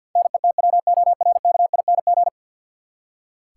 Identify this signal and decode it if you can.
Morse code, heard as a keyed tone, about 47 wpm, 700 Hz. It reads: NETWORKING
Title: Practice QSO - Exchange 3 - More Info - 50wpm